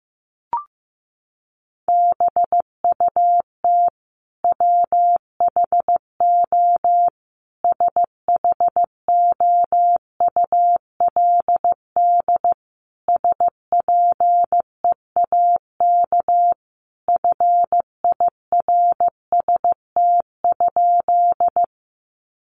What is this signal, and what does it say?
Morse code, heard as a keyed tone, about 15 wpm, 700 Hz. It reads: BUT WHO SHOULD SPEAK FIRST?
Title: Sentences from Top 1100 Words 15wpm